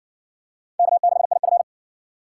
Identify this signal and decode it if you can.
Morse code, heard as a keyed tone, about 60 wpm, 700 Hz. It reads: K6IF